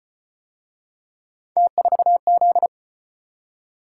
Morse code, heard as a keyed tone, about 34 wpm, 700 Hz. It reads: T4Z